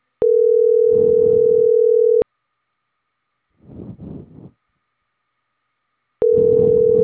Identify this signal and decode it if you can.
Ringback tone (North America), 440 + 480 Hz.